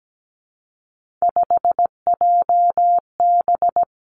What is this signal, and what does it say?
Morse code, heard as a keyed tone, about 17 wpm, 700 Hz. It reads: 5JB
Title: Three Challenging Letter-Number Combinations 17wpm